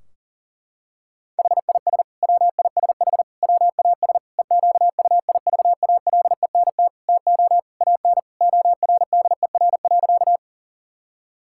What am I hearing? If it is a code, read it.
Morse code, heard as a keyed tone, about 40 wpm, 700 Hz. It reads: HIS WISH WAS EQUIVALENT TO AN ORDER.
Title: Sentences from Top 1600 Words 40wpm